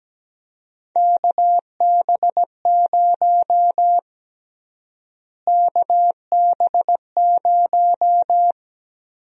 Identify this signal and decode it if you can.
Morse code, heard as a keyed tone, about 17 wpm, 700 Hz. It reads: KB0 KB0
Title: USA Call Sign Prefixes - Repeated 2x - 17wpm